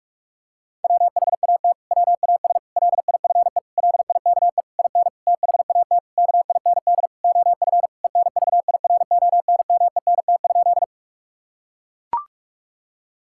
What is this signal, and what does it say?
Morse code, heard as a keyed tone, about 41 wpm, 700 Hz. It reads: WHAT WAS LIFE LIKE IN THAT KIND OF ENVIRONMENT?
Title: Sentences from Top 1000 Words - Review Part 5 - 45wpm